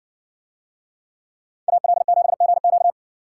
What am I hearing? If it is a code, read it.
Morse code, heard as a keyed tone, about 57 wpm, 700 Hz. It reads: AL7GQ